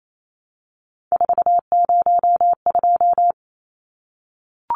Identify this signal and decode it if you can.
Morse code, heard as a keyed tone, about 28 wpm, 700 Hz. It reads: 402
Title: Numbers - 3-Digit 28wpm